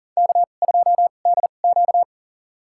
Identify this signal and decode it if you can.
Morse code, heard as a keyed tone, about 40 wpm, spaced as for 29 wpm, 700 Hz. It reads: K2DQ
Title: USA Call Signs - 4 Characters - 2x Character Spacing - 40wpm